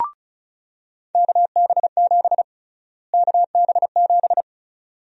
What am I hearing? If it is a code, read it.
Morse code, heard as a keyed tone, about 36 wpm, 700 Hz. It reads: KB7 KB7
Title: USA Call Sign Prefixes - Repeated 2x - 35wpm